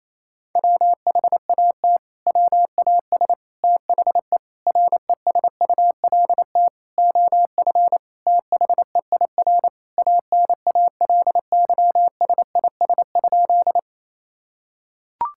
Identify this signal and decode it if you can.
Morse code, heard as a keyed tone, about 28 wpm, 700 Hz. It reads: WHAT WAS THE RESULT OF THEIR ANALYSIS?
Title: Sentences from Top 1000 Words - Review Part 5 - 28wpm